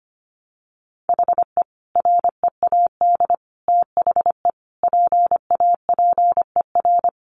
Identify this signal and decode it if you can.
Morse code, heard as a keyed tone, about 25 wpm, 700 Hz. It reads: HE READ THE PAPER
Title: Sentences from Top 1100 Words 25wpm